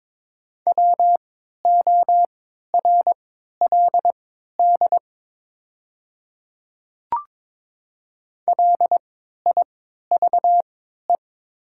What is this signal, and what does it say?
Morse code, heard as a keyed tone, about 22 wpm, spaced as for 12 wpm, 700 Hz. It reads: WORLD LIVE
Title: 3x Character Spacing - Top 500 Words - 22wpm